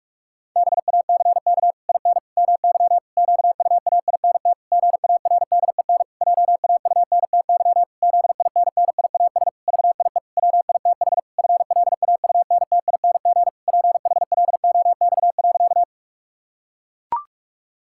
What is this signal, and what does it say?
Morse code, heard as a keyed tone, about 41 wpm, 700 Hz. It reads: BACK IN MY QUAINT GARDEN JAUNTY ZINNIAS VIE WITH FLAUNTING PHLOX.
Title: Pangram Challenge v2 - 45wpm